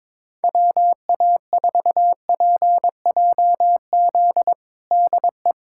Morse code, heard as a keyed tone, about 21 wpm, 700 Hz. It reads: WA4PJZ DE
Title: USA Callsign DE Callsign 22wpm